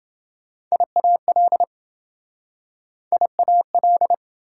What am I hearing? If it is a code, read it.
Morse code, heard as a keyed tone, about 26 wpm, 700 Hz. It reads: IAL IAL